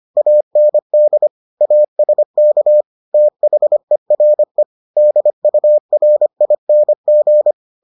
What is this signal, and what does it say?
Morse code, heard as a keyed tone, about 25 wpm, 600 Hz. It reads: AND ASK THERE DURING